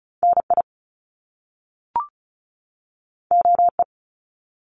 Morse code, heard as a keyed tone, about 34 wpm, 700 Hz. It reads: NI OE
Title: Top Two Letter Combinations in Words 35wpm